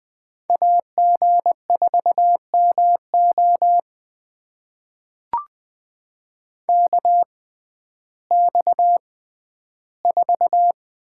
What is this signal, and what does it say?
Morse code, heard as a keyed tone, about 20 wpm, spaced as for 12 wpm, 700 Hz. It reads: AG4MO KX4
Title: USA Call Signs - 5 Characters - 6x Character Spacing - 20wpm